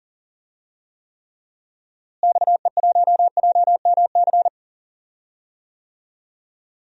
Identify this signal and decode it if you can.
Morse code, heard as a keyed tone, about 38 wpm, 700 Hz. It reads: XE1JMC
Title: DXCC-100 Call Signs - 6 Characters - Repeated 2x - 40wpm